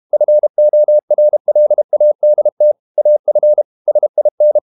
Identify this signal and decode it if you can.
Morse code, heard as a keyed tone, about 32 wpm, 600 Hz. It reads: FORLADT AF SIN
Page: Ferie 32 wpm